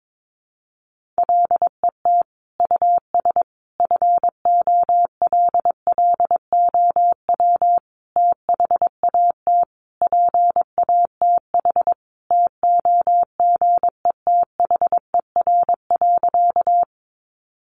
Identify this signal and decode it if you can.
Morse code, heard as a keyed tone, about 22 wpm, 700 Hz. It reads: LET US FOLLOW THAT PATH TOGETHER.
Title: Sentences from Top 1400 Words 22wpm